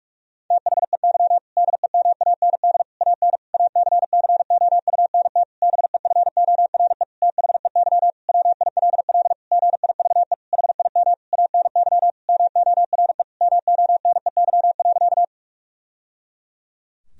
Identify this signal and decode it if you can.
Morse code, heard as a keyed tone, about 46 wpm, 700 Hz. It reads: THEY DEMAND AN ACCOUNT BEFORE THEY WILL GIVE HIM ANY MORE MONEY.
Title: Sentences from Top 1000 Words - Review Part 4 - 45wpm